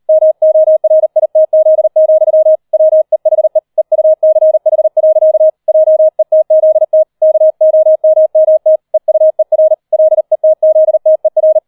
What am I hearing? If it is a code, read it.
Morse code, heard as a keyed tone, about 38 wpm, 630 Hz. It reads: MORITZ, WEHE EUCH. JETZT KOMMT EUER LETZTER